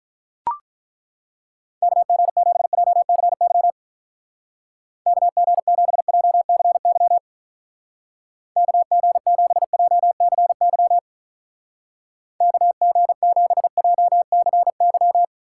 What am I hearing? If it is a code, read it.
Morse code, heard as a keyed tone, about 42 wpm, 700 Hz. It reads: KG7JCY KG7JCY KG7JCY KG7JCY
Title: Speed Racing - USA Call Signs 35wpm